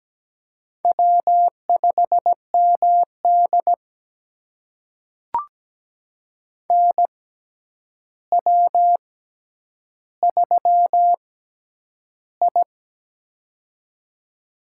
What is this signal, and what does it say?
Morse code, heard as a keyed tone, about 17 wpm, spaced as for 9 wpm, 700 Hz. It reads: W5MD NW3I
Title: USA Call Signs - 4 Characters - 6x Character Spacing - 17wpm